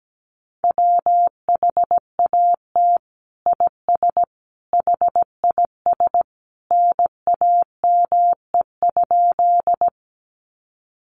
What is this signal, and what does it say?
Morse code, heard as a keyed tone, about 17 wpm, 700 Hz. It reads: WHAT IS HIS NAME?
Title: Sentences from Top 200 Words 17wpm